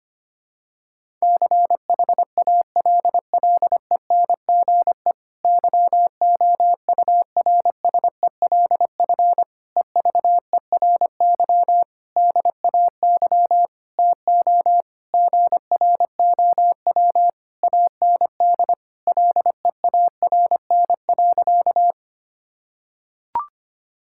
Morse code, heard as a keyed tone, about 25 wpm, 700 Hz. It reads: CHALLENGE YOURSELF EVERY DAY TO GROW AND LEARN.
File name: practice / 25wpm / Difficult Sentences with Hints - 25wpm